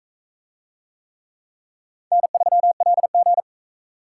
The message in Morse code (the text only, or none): N3LG